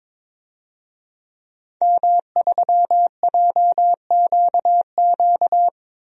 Morse code, heard as a keyed tone, about 22 wpm, 700 Hz. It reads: M3JQQ
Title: DXCC-100 Call Signs - 5 Characters - Repeated 3x - 22wpm